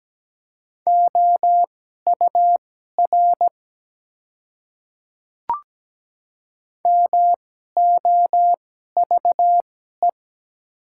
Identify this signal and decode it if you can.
Morse code, heard as a keyed tone, about 17 wpm, spaced as for 12 wpm, 700 Hz. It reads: OUR MOVE
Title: 2x Character Spacing - Top 200 Words - 17wpm